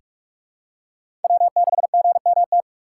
Morse code, heard as a keyed tone, about 44 wpm, 700 Hz. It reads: W6GMT
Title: USA Call Signs - 5 Characters - 6x Character Spacing - 45wpm